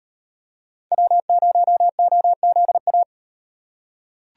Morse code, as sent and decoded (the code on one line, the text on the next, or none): .-- ----- --- --.. .-
W0OZA